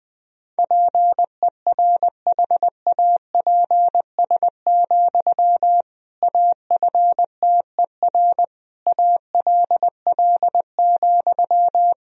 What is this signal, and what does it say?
Morse code, heard as a keyed tone, about 20 wpm, 700 Hz. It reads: PERHAPS, AFTER ALL,
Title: Sentences from Top 500 Words and Abbreviations 20wpm